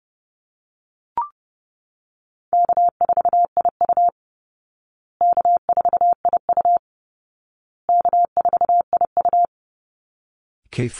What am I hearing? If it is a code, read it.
Morse code, heard as a keyed tone, about 28 wpm, 700 Hz. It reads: K4IU K4IU K4IU
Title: Easy Call Signs Repeated 3x - 30wpm